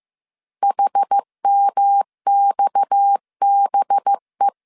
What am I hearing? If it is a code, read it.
Morse code, heard as a keyed tone, about 15 wpm, 800 Hz. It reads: HMXBE
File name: CW / DET / 1 6 1500C S C 14wpm 2951